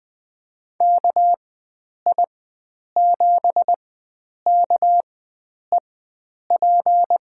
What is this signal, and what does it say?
Morse code, heard as a keyed tone, about 20 wpm, spaced as for 9 wpm, 700 Hz. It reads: KI7KEP
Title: USA Call Signs - 4x Character Spacing - 20wpm